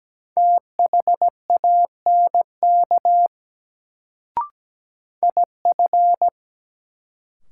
Morse code, heard as a keyed tone, about 17 wpm, 700 Hz. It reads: THANK IF
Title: Sets of 2 Words - New Word Drill - 17wpm